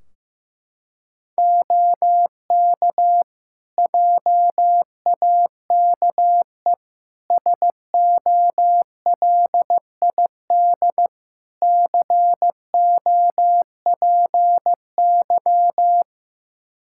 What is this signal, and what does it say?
Morse code, heard as a keyed tone, about 15 wpm, 700 Hz. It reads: OK JAKE SOLID COPY
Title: Practice QSO - Exchange 3 - More Info - 15wpm